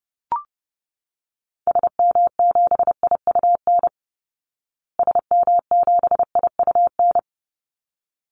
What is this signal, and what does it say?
Morse code, heard as a keyed tone, about 30 wpm, 700 Hz. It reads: SM7IUN SM7IUN
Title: Call Signs Repeated 3x - 30wpm